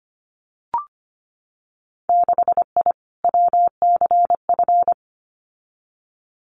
Morse code, heard as a keyed tone, about 25 wpm, 690 Hz. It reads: 6I WCF
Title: Winter Field Day Exchanges Encore 25wpm